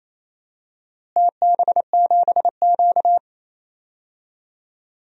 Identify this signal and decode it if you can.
Morse code, heard as a keyed tone, about 28 wpm, 700 Hz. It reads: TB7Q
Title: DXCC-100 Call Signs - 4 Characters - Repeated 2x - 28wpm